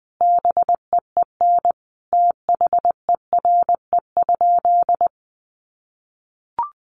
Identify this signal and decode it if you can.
Morse code, heard as a keyed tone, about 20 wpm, 700 Hz. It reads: BEEN THERE?
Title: Sentences from Top 100 Words 20wpm